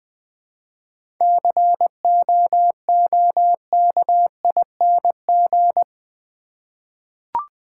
Morse code, heard as a keyed tone, about 20 wpm, 700 Hz. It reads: COOKING
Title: 7 Letter Words - First 3 Letters Given - 20wpm